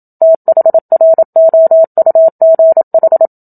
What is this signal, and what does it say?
Morse code, heard as a keyed tone, about 27 wpm, 650 Hz. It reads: THROUGH
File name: Book Two / 01-Chapter 1 - 27 WPM